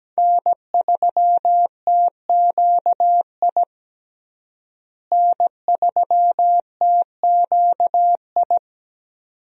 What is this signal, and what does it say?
Morse code, heard as a keyed tone, about 17 wpm, 700 Hz. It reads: N3TQI N3TQI
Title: USA Call Signs - 5 Characters - Repeated 2x - 17wpm